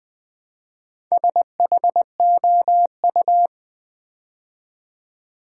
Morse code, heard as a keyed tone, about 20 wpm, 700 Hz. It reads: SHOU
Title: Build A Word - 2 Letters At A Time - 8-Letter Words - 20wpm